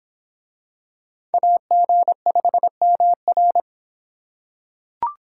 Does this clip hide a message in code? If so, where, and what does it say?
Yes: Morse code from 1.3 to 3.6 s, heard as a keyed tone, about 26 wpm, 700 Hz. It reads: AG5MR